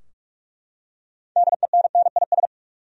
Morse code, heard as a keyed tone, about 42 wpm, 700 Hz. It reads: DENNIS
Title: USA Common Names - 45wpm